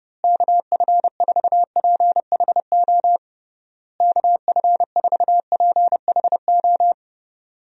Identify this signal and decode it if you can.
Morse code, heard as a keyed tone, about 30 wpm, 700 Hz. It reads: KF4PHO KF4PHO
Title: USA Call Signs - 6 Characters - Repeated 3x - 30wpm